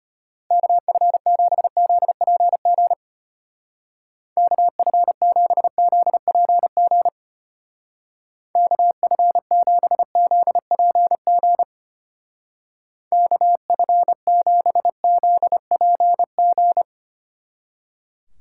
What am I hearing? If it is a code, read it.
Morse code, heard as a keyed tone, about 31 wpm, 700 Hz. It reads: KF7ZPG KF7ZPG KF7ZPG KF7ZPG
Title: Speed Racing - USA 2x3 Call Signs 25wpm